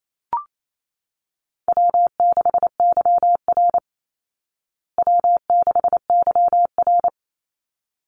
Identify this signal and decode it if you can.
Morse code, heard as a keyed tone, about 28 wpm, 700 Hz. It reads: W6YR W6YR
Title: Easy Call Signs Repeated 2x - 28wpm